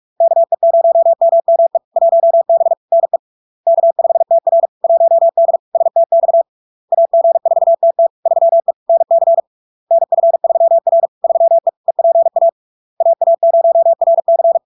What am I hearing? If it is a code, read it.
Morse code, heard as a keyed tone, about 46 wpm, 670 Hz. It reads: KE0MME 1B NE K5TR 1D STX AG4TT 3E NC NF3R 3E EPA AA0RC
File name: LARK-FD-2022-Log-in-CW-45wpm-670-Hz